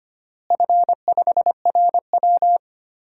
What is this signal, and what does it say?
Morse code, heard as a keyed tone, about 25 wpm, 700 Hz. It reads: F5RW
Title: DXCC-100 Call Signs - 4 Characters - Repeated 3x - 25wpm